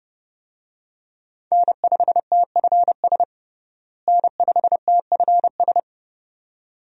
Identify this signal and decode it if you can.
Morse code, heard as a keyed tone, about 30 wpm, 700 Hz. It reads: N5TFS N5TFS
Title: USA Call Signs - 5 Characters - Repeated 2x - 30wpm